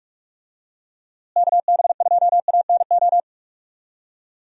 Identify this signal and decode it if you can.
Morse code, heard as a keyed tone, about 41 wpm, 700 Hz. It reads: KB2ANO
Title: USA Call Signs - 6 Characters - Repeated 3x - 45wpm